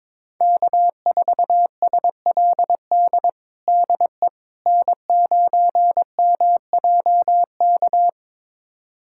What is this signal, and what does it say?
Morse code, heard as a keyed tone, about 22 wpm, 700 Hz. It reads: K4SLD DE N9MJK
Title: USA Callsign DE Callsign - 5 Characters - 22wpm